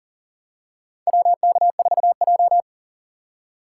Morse code, heard as a keyed tone, about 38 wpm, 700 Hz. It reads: WK4J
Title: USA Call Signs - 4 Characters - 3x Character Spacing - 40wpm